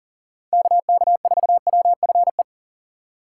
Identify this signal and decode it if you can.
Morse code, heard as a keyed tone, about 40 wpm, 700 Hz. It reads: KK4WFE
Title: USA Call Signs - 6 Characters - 2x Character Spacing - 40wpm